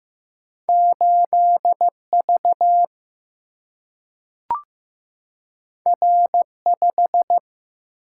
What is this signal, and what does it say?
Morse code, heard as a keyed tone, about 15 wpm, 700 Hz. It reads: 8V R5
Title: Single-Letter Single-Number 15wpm